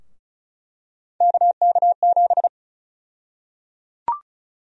Morse code, heard as a keyed tone, about 34 wpm, 700 Hz. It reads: KK7
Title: USA Call Sign Prefixes - 35wpm